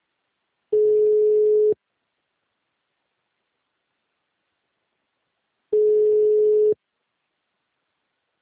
Ringback tone (Europe, 425 Hz).
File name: out- -2034- - - .96